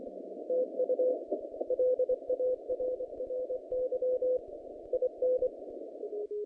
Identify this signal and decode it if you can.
Morse code, heard as a keyed tone, about 26 wpm, 510 Hz. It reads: TU LARRY IN
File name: contests / naqp-cw-k5tr-op-k5pi- -0000